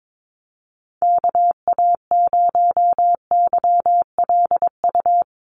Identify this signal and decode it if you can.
Morse code, heard as a keyed tone, about 22 wpm, 700 Hz. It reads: KA0YLU
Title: USA Call Signs Repeated 2x 22wpm